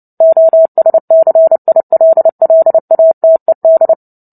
Morse code, heard as a keyed tone, about 30 wpm, 650 Hz. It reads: OSCILLATED
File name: Book Two / 03-Chapter 3 - 29 WPM